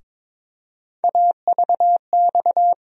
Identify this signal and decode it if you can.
Morse code, heard as a keyed tone, about 22 wpm, 700 Hz. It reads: AVX
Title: USA Call Sign Postfixes - 3 Letters 22wpm